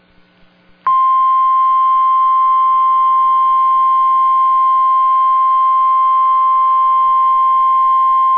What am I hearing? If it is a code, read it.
Emergency Alert System (SAME) tones (the 1050 Hz attention tone). It is heard from 0.9 to 7.8 s.